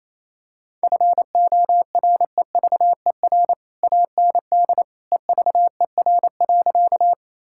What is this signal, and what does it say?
Morse code, heard as a keyed tone, about 28 wpm, 700 Hz. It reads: FOREVER AND EVER.
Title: Sentences from Top 1600 Words 28wpm